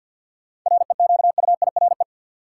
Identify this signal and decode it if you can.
Morse code, heard as a keyed tone, about 53 wpm, 700 Hz. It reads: REQUIRE
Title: Sentences from Top 900 Words - New Word Drill - 50wpm